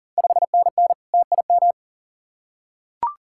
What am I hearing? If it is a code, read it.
Morse code, heard as a keyed tone, about 39 wpm, 700 Hz. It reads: 5NN TIM
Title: Ontario QSO Party Exchange 40wpm